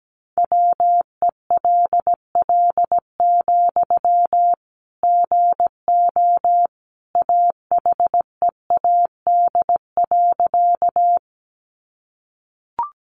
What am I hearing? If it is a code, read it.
Morse code, heard as a keyed tone, about 17 wpm, 700 Hz. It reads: WELL, GO AHEAD.
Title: Sentences from Top 1100 Words 17wpm